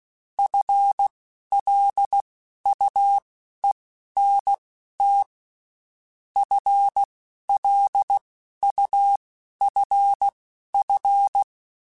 Morse code, heard as a keyed tone, about 16 wpm, spaced as for 11 wpm, 800 Hz. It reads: FLUENT FLUFF
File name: Spaced / FL Runs sp mp3s / FL Runs sp @15